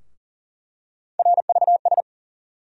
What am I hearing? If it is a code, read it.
Morse code, heard as a keyed tone, about 40 wpm, 700 Hz. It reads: RVS
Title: USA Call Sign Postfixes - 3 Letters 40wpm